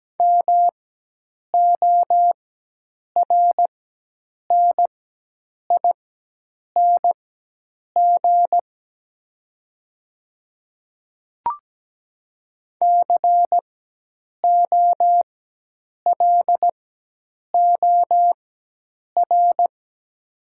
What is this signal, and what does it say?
Morse code, heard as a keyed tone, about 17 wpm, spaced as for 8 wpm, 700 Hz. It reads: MORNING COLOR
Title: 4x Character Spacing - Top 500 Words - 17wpm